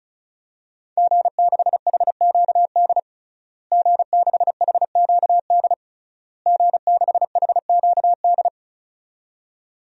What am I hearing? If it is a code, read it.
Morse code, heard as a keyed tone, about 36 wpm, 700 Hz. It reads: G6HQD G6HQD G6HQD
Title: DXCC-100 Call Signs - 5 Characters - Repeated 3x - 35wpm